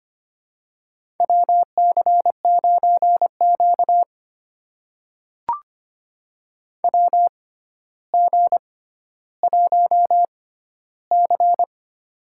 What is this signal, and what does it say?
Morse code, heard as a keyed tone, about 25 wpm, spaced as for 13 wpm, 700 Hz. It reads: WC9Q WG1C
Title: USA Call Signs - 4 Characters - 6x Character Spacing - 25wpm